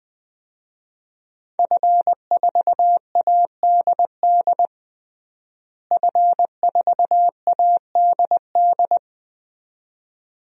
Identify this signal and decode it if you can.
Morse code, heard as a keyed tone, about 20 wpm, 700 Hz. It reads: F4ADD F4ADD
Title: DXCC-100 Call Signs - 5 Characters - Repeated 2x - 20wpm